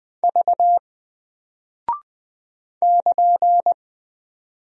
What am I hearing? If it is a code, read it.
Morse code, heard as a keyed tone, about 20 wpm, 700 Hz. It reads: V <KN>
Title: Single Character (Letter:Number:Punctuation:Prosign) Alt 20wpm